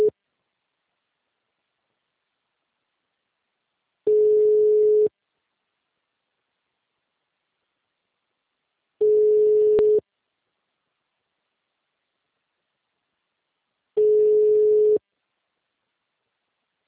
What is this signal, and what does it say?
Ringback tone (Europe, 425 Hz).